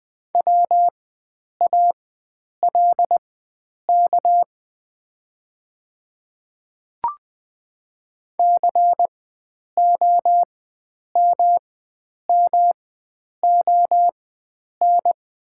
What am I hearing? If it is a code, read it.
Morse code, heard as a keyed tone, about 20 wpm, spaced as for 9 wpm, 700 Hz. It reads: WALK COMMON
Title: 4x Character Spacing - Top 500 Words - 20wpm